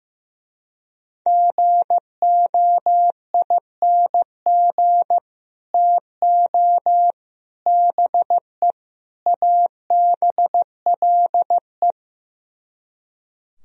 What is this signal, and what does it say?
Morse code, heard as a keyed tone, about 16 wpm, 700 Hz. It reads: GOING TO BE ABLE
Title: Speed-Racing - Sets of 4 Words 15wpm